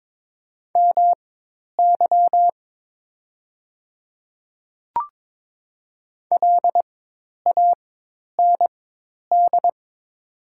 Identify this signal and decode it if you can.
Morse code, heard as a keyed tone, about 22 wpm, spaced as for 10 wpm, 700 Hz. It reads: MY LAND
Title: 4x Character Spacing - Top 200 Words - 22wpm